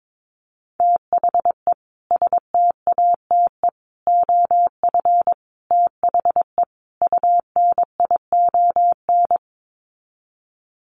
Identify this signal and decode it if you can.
Morse code, heard as a keyed tone, about 22 wpm, 700 Hz. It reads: THE STATE OF THE UNION
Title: Sets of 5 Words Encore 22wpm